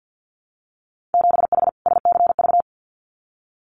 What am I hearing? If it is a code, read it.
Morse code, heard as a keyed tone, about 50 wpm, 700 Hz. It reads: 7H SCV
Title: Winter Field Day Exchanges Encore 50wpm